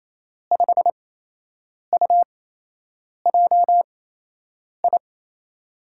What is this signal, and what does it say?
Morse code, heard as a keyed tone, about 27 wpm, spaced as for 8 wpm, 700 Hz. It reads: 5UJI
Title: USA Call Signs - 8x Character Spacing - 28wpm